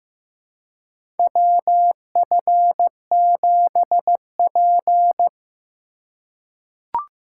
Morse code, heard as a keyed tone, about 15 wpm, 700 Hz. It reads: WF7P
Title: USA Call Signs - 4 Characters - 6x Character Spacing - 15wpm